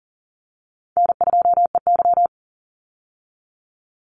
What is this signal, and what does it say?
Morse code, heard as a keyed tone, about 37 wpm, 700 Hz. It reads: N2EY
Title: Call Signs Repeated 2x - 40wpm